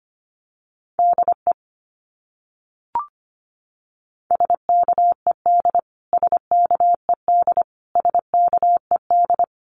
Morse code, heard as a keyed tone, about 25 wpm, 700 Hz. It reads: DE SKED SKED SKED
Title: Abbreviations Repeated 3x - 25wpm